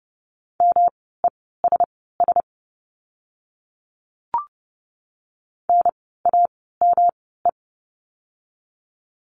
Morse code, heard as a keyed tone, about 30 wpm, spaced as for 17 wpm, 700 Hz. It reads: MESS NAME
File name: course / 30:10wpm / 108 Lesson 18f - Letter m - Words 30wpm